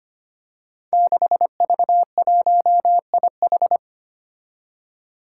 Morse code, heard as a keyed tone, about 25 wpm, 700 Hz. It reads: 6V1IH
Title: Five Challenging Letter-Number Combinations 25wpm